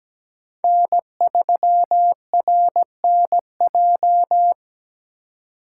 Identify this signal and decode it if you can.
Morse code, heard as a keyed tone, about 17 wpm, 700 Hz. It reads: N3RNJ
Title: USA Call Signs - 5 Characters - 2x Character Spacing - 17wpm